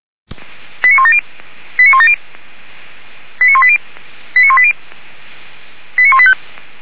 Five-tone selective call (ZVEI), heard as quick runs of five tones, tones of about 70 ms. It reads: 89170 89170 89180 89180 89185